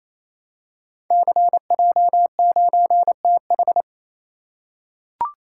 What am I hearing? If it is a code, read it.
Morse code, heard as a keyed tone, about 29 wpm, 700 Hz. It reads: CJ9TH
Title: Code Groups - 5 Letter:Numbers - 28wpm